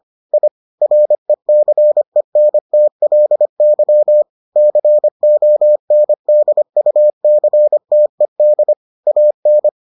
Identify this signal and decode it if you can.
Morse code, heard as a keyed tone, about 25 wpm, 600 Hz. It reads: I RECENTLY CONDUCTED AN